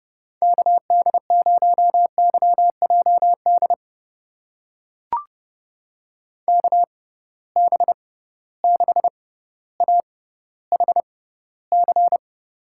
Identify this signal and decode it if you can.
Morse code, heard as a keyed tone, about 30 wpm, spaced as for 15 wpm, 700 Hz. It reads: KD0YJD KB6AHC
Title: USA Call Signs - 6 Characters - 6x Character Spacing - 30wpm